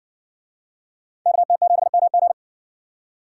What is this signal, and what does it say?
Morse code, heard as a keyed tone, about 57 wpm, 700 Hz. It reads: KT7MG